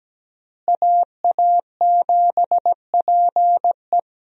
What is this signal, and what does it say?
Morse code, heard as a keyed tone, about 17 wpm, 700 Hz. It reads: AA7PE
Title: USA Call Signs - 5 Characters - Repeated 3x - 17wpm